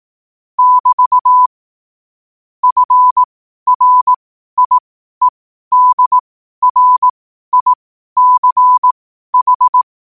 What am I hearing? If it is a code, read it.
Morse code, heard as a keyed tone, about 18 wpm, spaced as for 12 wpm, 1000 Hz. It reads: <BT> FRIEDRICH
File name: qotdfiles / 12wpm